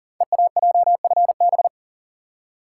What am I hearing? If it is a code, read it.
Morse code, heard as a keyed tone, about 38 wpm, 690 Hz. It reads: EAJFB
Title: Code Groups - 5 Letter:Numbers - 40wpm